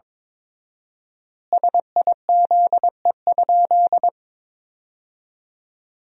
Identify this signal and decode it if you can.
Morse code, heard as a keyed tone, about 22 wpm, 700 Hz. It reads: SIZE?
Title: 4x Word-Spacing - Sentences from Top 500 Words - 22wpm